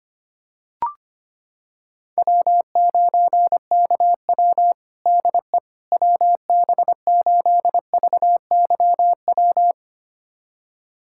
Morse code, heard as a keyed tone, about 25 wpm, 700 Hz. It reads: W9KW DE WB8VYW
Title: USA Callsign DE Callsign 25wpm